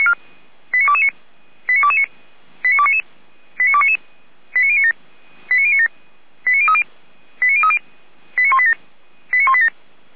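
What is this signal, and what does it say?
Five-tone selective call (ZVEI), heard as quick runs of five tones, tones of about 70 ms. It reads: 89229 89229 89299 89299 89097 89097 89030 89030 89187 89187
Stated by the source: Five-tone ZVEI call received by SDR, feuerwehr-orlamuende.org (fire brigade Orlamünde).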